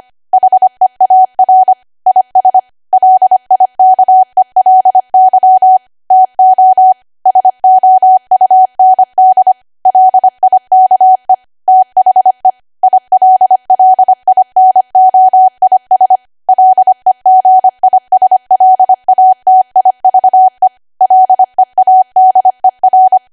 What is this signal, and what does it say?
Morse code, heard as a keyed tone, about 25 wpm, 750 Hz. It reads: HEAR IS LIKELY TO SOUND LIKE THE ILLINOIS LEGISLATIVE LEADER
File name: cwa / W1AW-Level3 / 25WPM